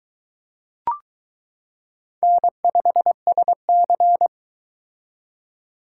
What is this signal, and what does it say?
Morse code, heard as a keyed tone, about 21 wpm, 700 Hz. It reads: N5SC